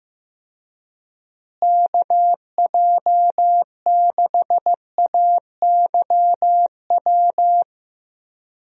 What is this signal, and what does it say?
Morse code, heard as a keyed tone, about 15 wpm, 700 Hz. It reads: KJ6AYW
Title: USA Call Signs - 6 Characters - Repeated 3x - 15wpm